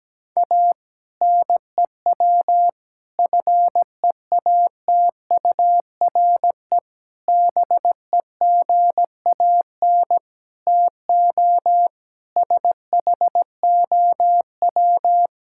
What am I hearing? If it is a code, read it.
Morse code, heard as a keyed tone, about 17 wpm, 700 Hz. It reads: A NEW FEATURE BEGAN TO SHOW
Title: Sentences from Top 1000 Words - Review Part 5 - 17wpm